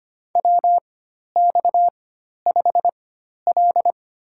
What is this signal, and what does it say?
Morse code, heard as a keyed tone, about 25 wpm, spaced as for 12 wpm, 700 Hz. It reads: WX5L
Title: USA Call Signs - 4x Character Spacing - 25wpm